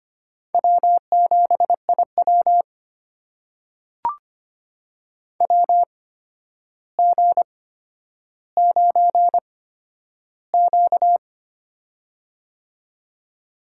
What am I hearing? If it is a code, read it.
Morse code, heard as a keyed tone, about 25 wpm, spaced as for 11 wpm, 700 Hz. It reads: W7IW WG9Q
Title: USA Call Signs - 4 Characters - 8x Character Spacing - 25wpm